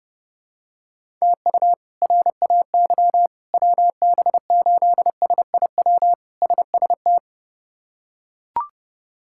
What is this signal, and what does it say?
Morse code, heard as a keyed tone, about 30 wpm, 700 Hz. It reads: TU RAY WB8SIW SST
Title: K1USN SST QSO Practice - 30wpm